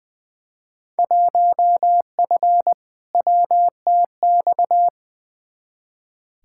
Morse code, heard as a keyed tone, about 20 wpm, 700 Hz. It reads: 1F WTX
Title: Field Day Exchanges Encore - 20wpm